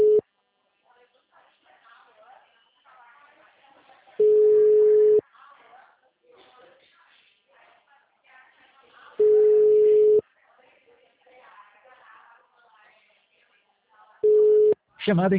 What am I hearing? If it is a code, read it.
Ringback tone (Europe, 425 Hz), with speech.
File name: out- -2012- - - .55321